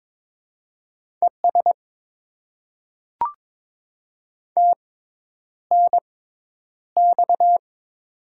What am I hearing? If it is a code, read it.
Morse code, heard as a keyed tone, about 21 wpm, spaced as for 10 wpm, 700 Hz. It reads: ES TNX
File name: Top 20 QSO Elements - 6x Character Spacing - 22wpm